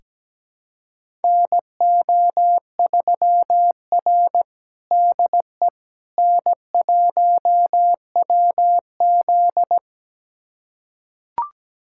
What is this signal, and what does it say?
Morse code, heard as a keyed tone, about 17 wpm, 700 Hz. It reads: NO3R DE N1WZ
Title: USA Callsign DE Callsign - 4 Characters - 17wpm